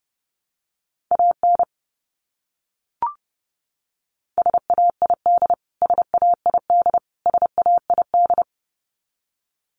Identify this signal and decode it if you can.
Morse code, heard as a keyed tone, about 30 wpm, 700 Hz. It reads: AN SAID SAID SAID
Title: Top 500 Words Repeated 30wpm